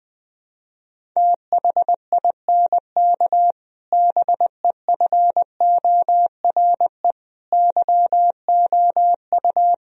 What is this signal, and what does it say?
Morse code, heard as a keyed tone, about 20 wpm, 700 Hz. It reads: THINK BEFORE YOU
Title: Sentences from Top 1000 Words - Review Part 3 - 20wpm